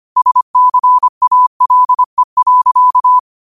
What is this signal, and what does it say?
Morse code, heard as a keyed tone, about 25 wpm, 1000 Hz. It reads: SCALE.
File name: qotdfiles / 25wpm